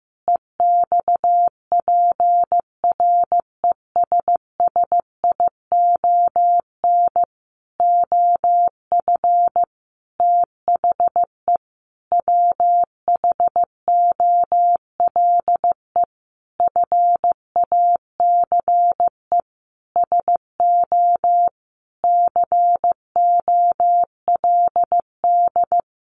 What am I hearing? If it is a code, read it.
Morse code, heard as a keyed tone, about 15 wpm, 700 Hz. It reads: EXPRESSION OF THE WHOLE FACE SO COLD
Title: Sentences from Top 1400 Words 15wpm